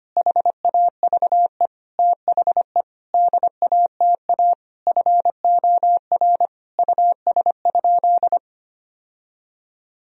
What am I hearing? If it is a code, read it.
Morse code, heard as a keyed tone, about 25 wpm, 700 Hz. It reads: HAVE THE DATA FOR US?
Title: Sentences from Top 1000 Words - Review Part 3 - 25wpm